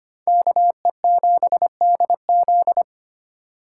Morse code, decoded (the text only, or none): KE7DZ